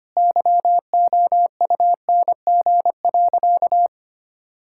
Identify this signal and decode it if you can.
Morse code, heard as a keyed tone, about 25 wpm, 700 Hz. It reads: YOUNG.
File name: practice / 25wpm / Sentences with 3-Word Phrases - 25wpm